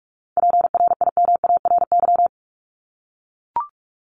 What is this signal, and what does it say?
Morse code, heard as a keyed tone, about 44 wpm, 700 Hz. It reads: PRIMARY
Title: Top 501-750 Words 45wpm